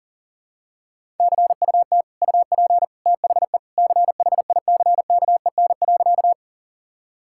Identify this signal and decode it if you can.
Morse code, heard as a keyed tone, about 40 wpm, 700 Hz. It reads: CUT UP THE CHICKEN.
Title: Sentences from Top 1000 Words - Review Part 5 - 40wpm